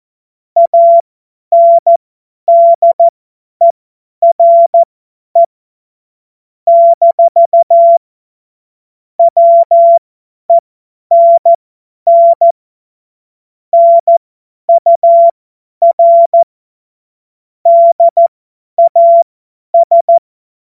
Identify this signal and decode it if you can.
Morse code, heard as a keyed tone, about 14 wpm, spaced as for 10 wpm, 680 Hz. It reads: ANDERE - WENN NUR DAS
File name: Morsekurs / Witze / Witz 13. Juli 2021, 50 F70